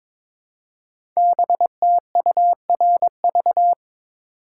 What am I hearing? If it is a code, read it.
Morse code, heard as a keyed tone, about 22 wpm, 700 Hz. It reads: BTURV
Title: Code Groups - 5 Letters - 22wpm